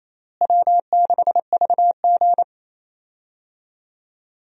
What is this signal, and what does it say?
Morse code, heard as a keyed tone, about 28 wpm, 700 Hz. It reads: W6VG
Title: USA Call Signs - 4 Characters - Repeated 2x - 28wpm